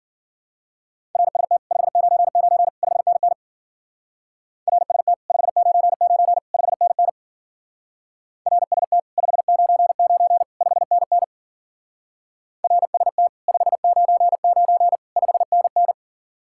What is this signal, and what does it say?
Morse code, heard as a keyed tone, about 50 wpm, 700 Hz. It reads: RST 599 5NN RST 599 5NN RST 599 5NN RST 599 5NN
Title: Speed-Racing Common QSO Phrases Encore - 40wpm